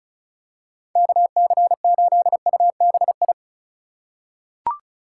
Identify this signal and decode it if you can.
Morse code, heard as a keyed tone, about 36 wpm, 700 Hz. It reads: KC8UBI
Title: USA Call Signs - 6 Characters - 3x Character Spacing - 35wpm